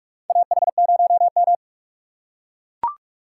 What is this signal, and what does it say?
Morse code, heard as a keyed tone, about 43 wpm, 700 Hz. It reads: AH0M
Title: USA Call Signs - 4 Characters - 4x Character Spacing - 45wpm